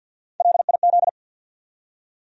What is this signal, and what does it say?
Morse code, heard as a keyed tone, about 52 wpm, 700 Hz. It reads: LIZ
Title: USA Common Names - Max 4 Letters - 50wpm